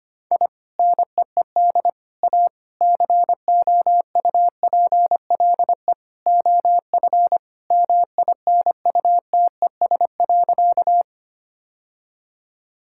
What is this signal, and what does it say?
Morse code, heard as a keyed tone, about 25 wpm, 700 Hz. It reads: I NEED A COUPLE OF MINUTES.
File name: practice / 25wpm / Sentences with 3-Word Phrases - 25wpm